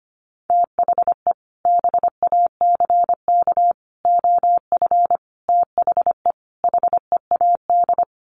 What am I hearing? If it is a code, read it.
Morse code, heard as a keyed tone, about 25 wpm, 700 Hz. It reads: THE BACK OF THE HEAD
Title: Sets of 5 Words Encore2 25wpm